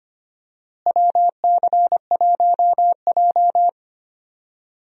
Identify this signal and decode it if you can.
Morse code, heard as a keyed tone, about 25 wpm, 700 Hz. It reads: WC1J
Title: USA Call Signs - 4 Characters - Repeated 3x - 25wpm